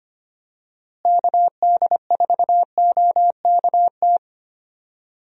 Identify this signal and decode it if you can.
Morse code, heard as a keyed tone, about 25 wpm, 700 Hz. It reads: KD4OKT
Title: USA Call Signs - 6 Characters - Repeated 3x - 25wpm